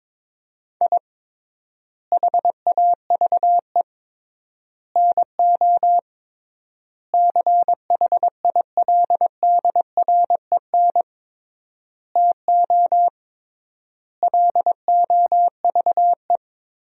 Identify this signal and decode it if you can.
Morse code, heard as a keyed tone, about 22 wpm, 700 Hz. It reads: I HAVE NO CHILDREN TO LOVE
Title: 3x Word-Spacing - Sentences from Top 400 Words - 22wpm